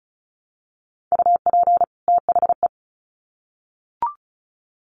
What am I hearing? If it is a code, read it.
Morse code, heard as a keyed tone, about 34 wpm, 700 Hz. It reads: UP THE
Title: Sets of 2 Words 35wpm